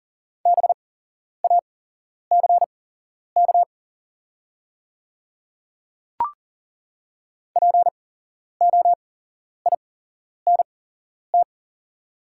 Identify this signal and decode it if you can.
Morse code, heard as a keyed tone, about 38 wpm, spaced as for 11 wpm, 690 Hz. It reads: BACK POINT
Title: 8x Character Spacing - Top 300 Words - 40wpm